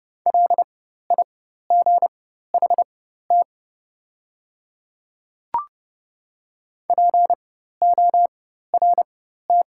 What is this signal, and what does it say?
Morse code, heard as a keyed tone, about 30 wpm, spaced as for 14 wpm, 700 Hz. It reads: LIGHT PORT
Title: 4x Character Spacing - Top 200 Words - 30wpm